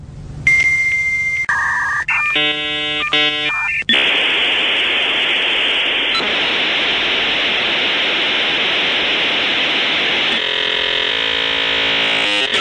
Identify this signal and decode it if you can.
Dial-up modem handshake.